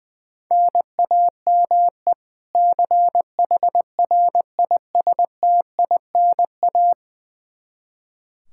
Morse code, heard as a keyed tone, about 20 wpm, 700 Hz. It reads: NAME CHRISTINA
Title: Practice QSO - Exchange 1 - Beginning - 20wpm